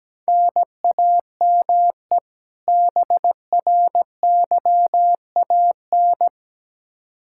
Morse code, heard as a keyed tone, about 17 wpm, 700 Hz. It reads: NAME BRYAN
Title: K1USN SST QSO Practice 17wpm